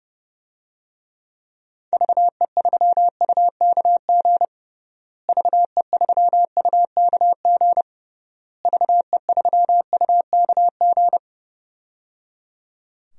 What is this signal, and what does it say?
Morse code, heard as a keyed tone, about 30 wpm, 700 Hz. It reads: VE3UKG VE3UKG VE3UKG
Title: DXCC-100 Call Signs - 6 Characters - Repeated 3x - 30wpm